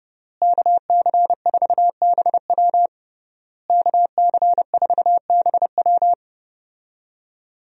Morse code, heard as a keyed tone, about 30 wpm, 700 Hz. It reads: KC4BW KC4BW
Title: USA Call Signs - 5 Characters - Repeated 3x - 30wpm